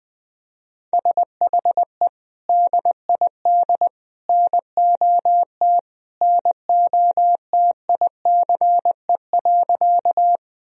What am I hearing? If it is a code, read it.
Morse code, heard as a keyed tone, about 21 wpm, 700 Hz. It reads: SHE DID NOT NOTICE.
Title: Sentences from Top 1000 Words - Review Part 2 - 20wpm